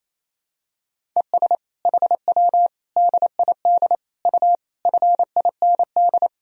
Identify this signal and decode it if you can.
Morse code, heard as a keyed tone, about 28 wpm, 700 Hz. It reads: ES HW DID U FIND
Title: Sentences from Top 100 Words and Abbreviations 28wpm